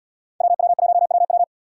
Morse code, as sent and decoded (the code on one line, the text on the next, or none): .-- .-- .---- .-- .--
WW1WW